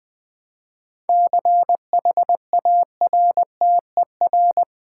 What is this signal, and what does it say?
Morse code, heard as a keyed tone, about 20 wpm, 700 Hz. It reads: CHARTER
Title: 7 Letter Words - First 3 Letters Given - 20wpm